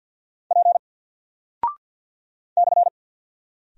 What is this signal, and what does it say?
Morse code, heard as a keyed tone, about 50 wpm, 700 Hz. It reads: P /
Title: Single Character (Letter:Number:Punctuation:Prosign) Alt 50wpm